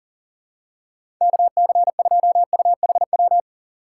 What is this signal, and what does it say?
Morse code, heard as a keyed tone, about 40 wpm, 700 Hz. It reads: KC2UHW
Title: USA Call Signs - 2x Character Spacing - 40wpm